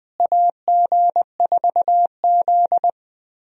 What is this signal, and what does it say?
Morse code, heard as a keyed tone, about 21 wpm, 700 Hz. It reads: AG4Z
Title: USA Call Signs - 4 Characters - Repeated 2x - 20wpm